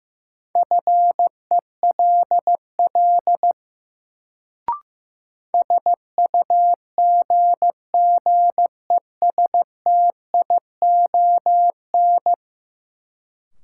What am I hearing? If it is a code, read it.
Morse code, heard as a keyed tone, about 15 wpm, 700 Hz. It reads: FELL SUGGESTION
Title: Sentences from Top 1600 Words - New Word Drill Alt - 15wpm